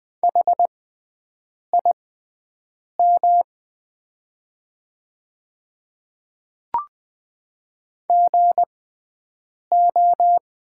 Morse code, heard as a keyed tone, about 20 wpm, spaced as for 7 wpm, 700 Hz. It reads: HIM GO
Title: 6x Character Spacing - Top 100 Words - 20wpm